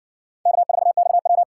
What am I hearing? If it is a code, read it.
Morse code, heard as a keyed tone, about 52 wpm, 700 Hz. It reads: K4XW